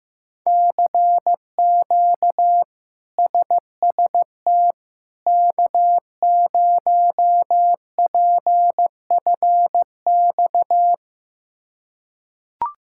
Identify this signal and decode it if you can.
Morse code, heard as a keyed tone, about 15 wpm, 700 Hz. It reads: CQ SST K0PFX
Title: K1USN SST QSO Practice - 15wpm